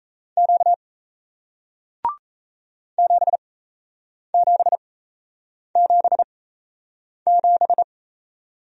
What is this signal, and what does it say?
Morse code, heard as a keyed tone, about 37 wpm, spaced as for 8 wpm, 700 Hz. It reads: Q 7777